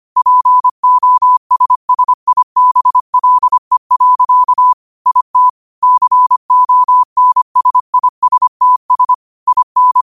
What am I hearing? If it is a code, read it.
Morse code, heard as a keyed tone, about 25 wpm, 1000 Hz. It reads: POSSIBLE. IT CONSISTS IN
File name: qotdfiles / 25wpm